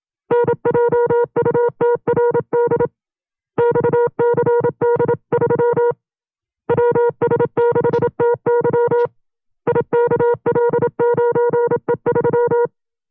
Morse code, heard as a keyed tone, about 27 wpm, 480 Hz. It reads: NJUTRD XCD3 WS6TY IKL9E3
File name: FreeFiles / CW-practice / 25WPM-side1